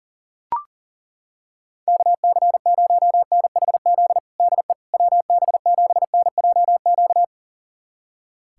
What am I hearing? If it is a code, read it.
Morse code, heard as a keyed tone, about 40 wpm, 690 Hz. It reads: KC0NHZ DE WB7NJQ
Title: USA Callsign DE Callsign - 6 Characters - 40wpm